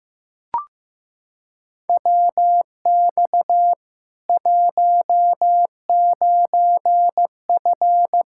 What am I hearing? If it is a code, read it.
Morse code, heard as a keyed tone, about 15 wpm, 700 Hz. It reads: WX 19F
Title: K1USN SST QSO Practice 15wpm